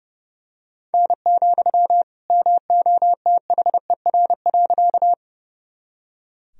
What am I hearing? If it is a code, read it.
Morse code, heard as a keyed tone, about 32 wpm, 700 Hz. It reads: N, MOTHER.
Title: Sentences from Top 200 Words and Abbreviations 30wpm